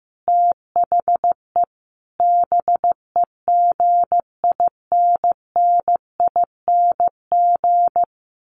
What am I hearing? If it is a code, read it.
Morse code, heard as a keyed tone, about 15 wpm, 700 Hz. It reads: THE BEGINNING
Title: Sentences from Top 1400 Words 15wpm